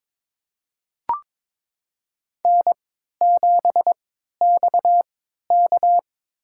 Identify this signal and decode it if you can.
Morse code, heard as a keyed tone, about 22 wpm, spaced as for 12 wpm, 700 Hz. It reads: N7XK
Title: USA Call Signs - 4 Characters - 3x Character Spacing - 22wpm